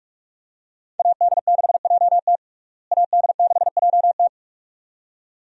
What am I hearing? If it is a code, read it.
Morse code, heard as a keyed tone, about 46 wpm, 700 Hz. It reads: AD6JT AD6JT
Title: USA Call Signs - 5 Characters - Repeated 2x - 45wpm